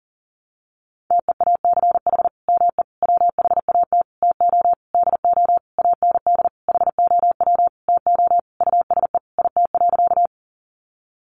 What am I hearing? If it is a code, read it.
Morse code, heard as a keyed tone, about 40 wpm, 700 Hz. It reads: TEACH ME WHAT TO DO AND HOW TO USE IT.
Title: Sentences from Top 500 Words 40wpm